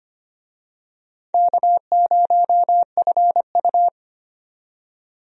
Morse code, heard as a keyed tone, about 25 wpm, 700 Hz. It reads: K0FU
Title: USA Call Signs - 4 Characters - 25wpm